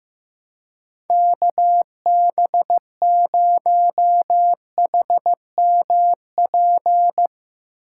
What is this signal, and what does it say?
Morse code, heard as a keyed tone, about 15 wpm, 700 Hz. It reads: KB0HMP
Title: USA Call Signs - 6 Characters - 3x Character Spacing - 15wpm